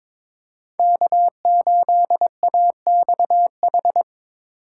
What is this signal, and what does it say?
Morse code, heard as a keyed tone, about 22 wpm, 700 Hz. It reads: K8AXH
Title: USA Call Signs - 5 Characters - 4x Character Spacing - 22wpm